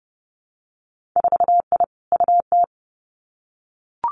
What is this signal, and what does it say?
Morse code, heard as a keyed tone, about 30 wpm, 700 Hz. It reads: 4I UT
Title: Winter Field Day Exchanges Encore 30wpm